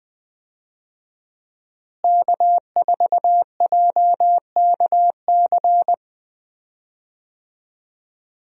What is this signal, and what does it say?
Morse code, heard as a keyed tone, about 20 wpm, 700 Hz. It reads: K4JKC
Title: USA Call Signs - 5 Characters - Repeated 3x - 20wpm